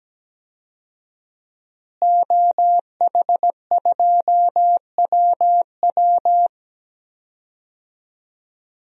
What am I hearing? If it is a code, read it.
Morse code, heard as a keyed tone, about 17 wpm, 700 Hz. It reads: OH2WW
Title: DXCC-100 Call Signs - 5 Characters - Repeated 3x - 17wpm